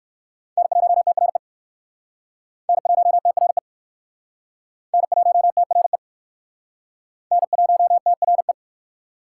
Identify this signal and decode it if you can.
Morse code, heard as a keyed tone, about 60 wpm, 700 Hz. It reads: N1TRE N1TRE N1TRE N1TRE